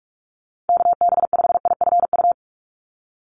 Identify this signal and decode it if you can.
Morse code, heard as a keyed tone, about 46 wpm, 700 Hz. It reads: KB5IFU
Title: USA Call Signs 45wpm